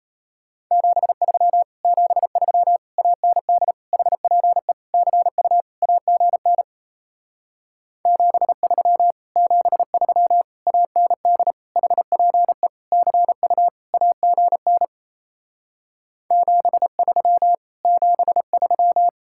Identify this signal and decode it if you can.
Morse code, heard as a keyed tone, about 38 wpm, 700 Hz. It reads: 73 73 AND HPE CU AGN 73 73 AND HPE CU AGN 73 73